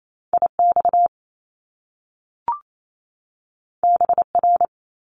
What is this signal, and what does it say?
Morse code, heard as a keyed tone, about 28 wpm, 700 Hz. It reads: IX BR
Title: Top Two Letter Combinations in Words 28wpm